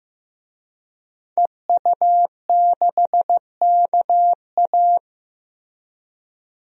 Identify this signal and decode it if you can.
Morse code, heard as a keyed tone, about 15 wpm, 700 Hz. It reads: EU6KA
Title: DXCC-100 Call Signs - 5 Characters - Repeated 2x - 15wpm